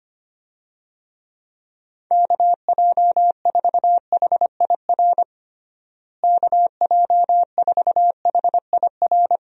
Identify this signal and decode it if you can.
Morse code, heard as a keyed tone, about 25 wpm, 700 Hz. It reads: KJ4HIR KJ4HIR
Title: USA Call Signs - 6 Characters - Repeated 3x - 25wpm